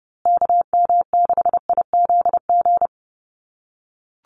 Morse code, heard as a keyed tone, about 30 wpm, 700 Hz. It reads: KM6IZG
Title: USA Call Signs Repeated 3x 30wpm